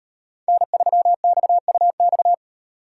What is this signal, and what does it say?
Morse code, heard as a keyed tone, about 36 wpm, 700 Hz. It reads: N3XUX